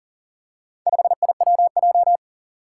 Morse code, heard as a keyed tone, about 40 wpm, 700 Hz. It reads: 5IWJ
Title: Four Challenging Letter-Number Combinations 40wpm